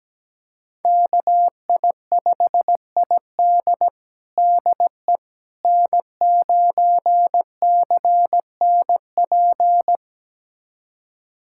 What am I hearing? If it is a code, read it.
Morse code, heard as a keyed tone, about 17 wpm, 700 Hz. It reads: KI5ID DE N9CNP
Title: USA Callsign DE Callsign - 5 Characters - 17wpm